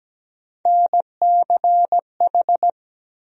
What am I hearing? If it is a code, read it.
Morse code, heard as a keyed tone, about 17 wpm, 700 Hz. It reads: NCH
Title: USA Call Sign Postfixes - 3 Letters 17wpm